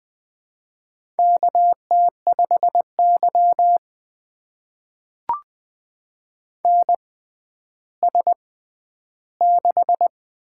Morse code, heard as a keyed tone, about 20 wpm, spaced as for 11 wpm, 700 Hz. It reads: KT5Y NS6
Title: USA Call Signs - 4 Characters - 6x Character Spacing - 20wpm